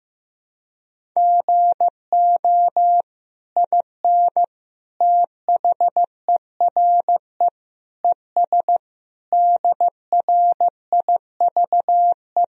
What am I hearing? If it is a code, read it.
Morse code, heard as a keyed tone, about 15 wpm, 700 Hz. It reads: GO IN THERE ES DRIVE
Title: Sentences from Top 500 Words and Abbreviations 15wpm